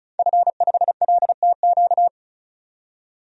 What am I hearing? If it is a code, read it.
Morse code, heard as a keyed tone, about 36 wpm, 700 Hz. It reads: F5LTQ
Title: DXCC-100 Call Signs - 5 Characters - 35wpm